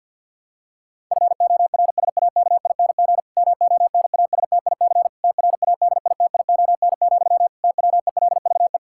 Morse code, heard as a keyed tone, about 49 wpm, 700 Hz. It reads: FORSAKING MONASTIC TRADITION, TWELVE
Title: Pangram Challenge v2 - 50wpm